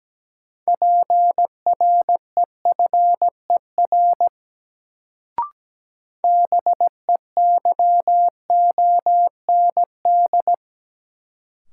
Morse code, heard as a keyed tone, about 17 wpm, 700 Hz. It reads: PREFER BEYOND
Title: Sentences from Top 1100 Words - New Word Drill Alt - 17wpm